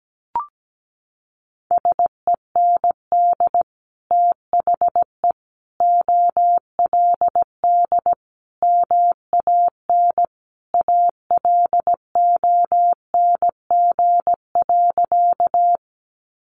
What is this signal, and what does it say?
Morse code, heard as a keyed tone, about 17 wpm, 700 Hz. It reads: SEND THE OLD MAN ALONG.
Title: Sentences from Top 1500 Words 17wpm